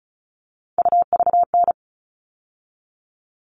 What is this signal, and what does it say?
Morse code, heard as a keyed tone, about 34 wpm, 700 Hz. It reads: UVN
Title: All Cut Numbers - 3-Digit 35wpm